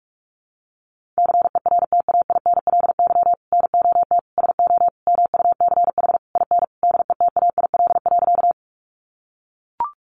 Morse code, heard as a keyed tone, about 44 wpm, 700 Hz. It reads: CERTAINLY NOT SO MUCH IN DETAIL.
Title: Sentences from Top 1500 Words 45wpm